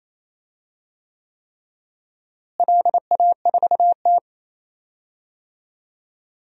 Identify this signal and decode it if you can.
Morse code, heard as a keyed tone, about 27 wpm, 700 Hz. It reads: LA4T
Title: DXCC-100 Call Signs - 4 Characters - Repeated 3x - 28wpm